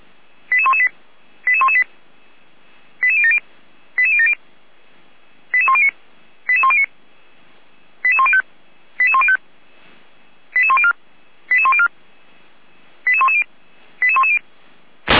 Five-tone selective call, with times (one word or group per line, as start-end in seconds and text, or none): 0.5-0.9 88118
1.5-1.8 88118
3.0-3.4 88070
4.0-4.3 88070
5.5-5.9 88109
6.5-6.9 88109
8.0-8.4 88105
9.0-9.4 88105
10.6-10.9 88104
11.5-11.9 88104
13.1-13.4 88110
14.0-14.4 88110